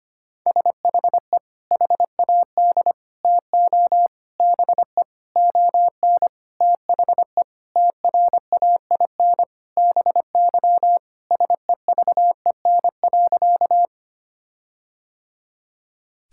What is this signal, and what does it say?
Morse code, heard as a keyed tone, about 25 wpm, 700 Hz. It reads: SHE HAD TO BE ON THE TRAIN BY SEVEN.
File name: practice / 25wpm / Sentences with 4-Word Phrases - 25wpm